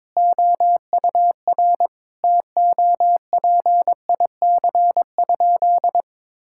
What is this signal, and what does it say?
Morse code, heard as a keyed tone, about 22 wpm, 700 Hz. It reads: OUR TOPIC?
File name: practice / 22wpm / Sentences with 8-Word Phrases - 22wpm